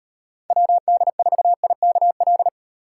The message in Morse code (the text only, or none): WD4IKL